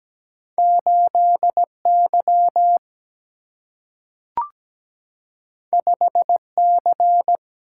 Morse code, heard as a keyed tone, about 17 wpm, 700 Hz. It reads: 8 Y 5 C
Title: Single-Letter Single-Number 17wpm